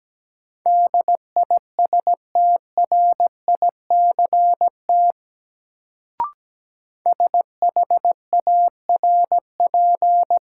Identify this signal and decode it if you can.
Morse code, heard as a keyed tone, about 17 wpm, 700 Hz. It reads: DISTRICT SHARP
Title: Sentences from Top 1600 Words - New Word Drill Alt - 17wpm